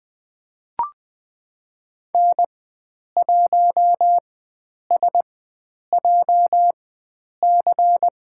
Morse code, heard as a keyed tone, about 20 wpm, spaced as for 9 wpm, 700 Hz. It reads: N1SJC
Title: USA Call Signs - 5 Characters - 4x Character Spacing - 20wpm